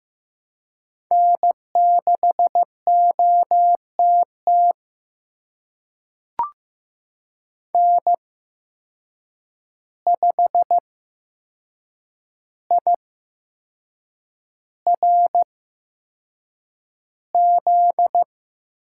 Morse code, heard as a keyed tone, about 15 wpm, spaced as for 6 wpm, 700 Hz. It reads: N6OTT N5IRZ
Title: USA Call Signs - 5 Characters - 8x Character Spacing - 15wpm